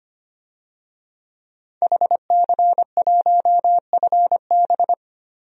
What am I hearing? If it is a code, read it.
Morse code, heard as a keyed tone, about 25 wpm, 700 Hz. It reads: HC1FB
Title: DXCC-100 Call Signs - 5 Characters - 25wpm